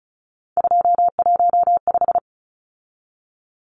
Morse code, heard as a keyed tone, about 34 wpm, 700 Hz. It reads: 215
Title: Numbers - 3-Digit 35wpm